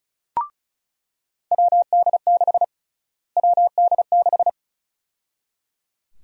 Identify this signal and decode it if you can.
Morse code, heard as a keyed tone, about 34 wpm, 700 Hz. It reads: WD6 WD6
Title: USA Call Sign Prefixes - Repeated 2x - 35wpm